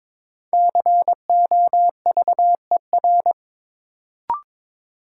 Morse code, heard as a keyed tone, about 21 wpm, 700 Hz. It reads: COVER
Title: Sentences from Top 500 Words - New Word Drill - 22wpm